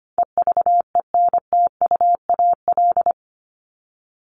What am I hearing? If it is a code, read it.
Morse code, heard as a keyed tone, about 25 wpm, 700 Hz. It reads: EVENTUAL
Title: LY Suffix - Top 100 Words with Base Words 25wpm